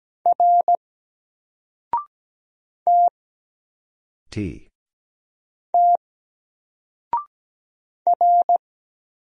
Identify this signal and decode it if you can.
Morse code, heard as a keyed tone, about 17 wpm, 700 Hz. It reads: R T T R
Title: Single Letter-Number Alt 17wpm